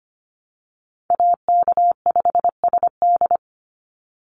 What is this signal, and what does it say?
Morse code, heard as a keyed tone, about 25 wpm, 700 Hz. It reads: AK5SD
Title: NA SOTA Chaser Callsigns 25wpm